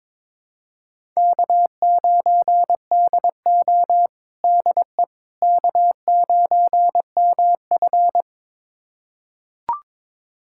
Morse code, heard as a keyed tone, about 21 wpm, 700 Hz. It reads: K9DO DE K9MF
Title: USA Callsign DE Callsign - 4 Characters - 22wpm